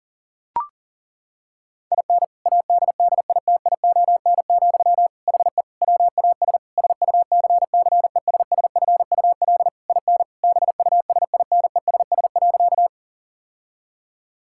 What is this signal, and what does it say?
Morse code, heard as a keyed tone, about 40 wpm, 700 Hz. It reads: IN ADDITION, HE WAS SUCCESSFUL IN BUSINESS.
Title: Sentences from Top 1000 Words - Review Part 6 - 40wpm